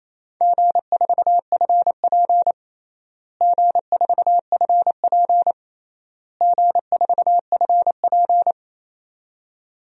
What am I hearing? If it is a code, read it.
Morse code, heard as a keyed tone, about 28 wpm, 700 Hz. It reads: G4FP G4FP G4FP
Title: DXCC-100 Call Signs - 4 Characters - Repeated 3x - 28wpm